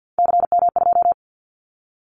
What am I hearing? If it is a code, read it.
Morse code, heard as a keyed tone, about 50 wpm, 700 Hz. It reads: CM2
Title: Callsign Prefixes for Top 100 DXCC - 50wpm